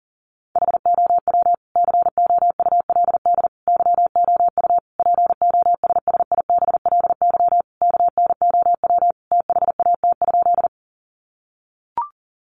Morse code, heard as a keyed tone, about 40 wpm, 700 Hz. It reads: HOW COULD YOU POSSIBLY KNOW THAT?
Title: Sentences from Top 1300 Words 40wpm